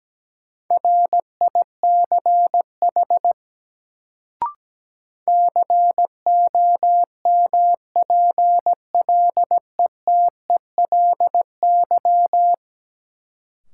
Morse code, heard as a keyed tone, about 17 wpm, 700 Hz. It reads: RICH COMPLETELY
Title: Sentences from Top 1100 Words - New Word Drill Alt - 17wpm